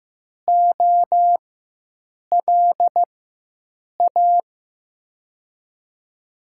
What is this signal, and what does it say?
Morse code, heard as a keyed tone, about 15 wpm, spaced as for 7 wpm, 700 Hz. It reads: OLA
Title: USA Call Signs - 4x Character Spacing - 15wpm